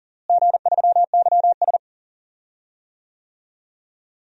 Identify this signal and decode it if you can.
Morse code, heard as a keyed tone, about 40 wpm, 700 Hz. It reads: G3YS
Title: DXCC-100 Call Signs - 4 Characters - Repeated 2x - 40wpm